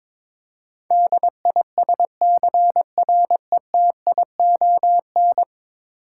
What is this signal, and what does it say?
Morse code, heard as a keyed tone, about 22 wpm, 700 Hz. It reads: DISCRETION
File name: practice / 22wpm / 10 Letter Words - First 3 Letters Given - 22wpm